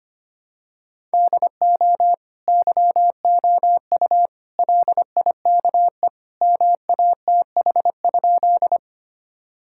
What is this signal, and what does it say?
Morse code, heard as a keyed tone, about 25 wpm, 700 Hz. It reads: DO YOU LIKE MATH?
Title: Sentences from Top 1000 Words - Review Part 6 - 25wpm